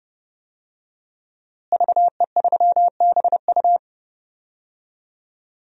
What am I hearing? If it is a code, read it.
Morse code, heard as a keyed tone, about 30 wpm, 700 Hz. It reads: VE3BU
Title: DXCC-100 Call Signs - 5 Characters - Repeated 3x - 30wpm